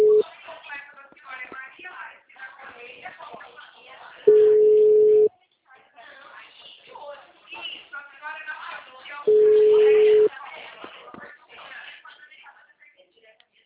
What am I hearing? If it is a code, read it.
Ringback tone (Europe, 425 Hz), with speech.